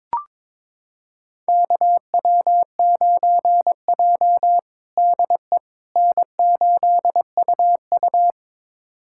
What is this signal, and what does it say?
Morse code, heard as a keyed tone, about 22 wpm, 700 Hz. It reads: KW9J DE N8UU
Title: USA Callsign DE Callsign - 4 Characters - 22wpm